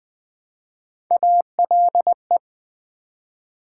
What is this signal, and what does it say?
Morse code, heard as a keyed tone, about 20 wpm, 700 Hz. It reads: ALE
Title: Build A Word - 1 Letter At A Time - 3 to 5-Letter Words - 20wpm